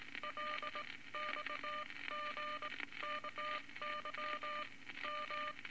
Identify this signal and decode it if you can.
Morse code, heard as a keyed tone, about 19 wpm, spaced as for 16 wpm, 1200 Hz, with noise. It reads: LXGKY M
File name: Morse Code Records / 19 B LCM 167 Twelve Words Per Minute